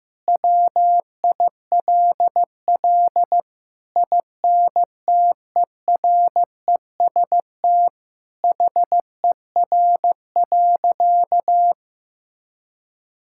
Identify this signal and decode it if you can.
Morse code, heard as a keyed tone, about 15 wpm, 700 Hz. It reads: WILL INTEREST HER.
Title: Sentences from Top 1000 Words - Review Part 5 - 15wpm